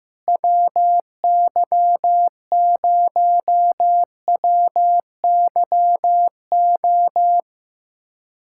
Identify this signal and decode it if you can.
Morse code, heard as a keyed tone, about 15 wpm, 700 Hz. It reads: WY0WYO
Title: SOTA QSO Practice 15wpm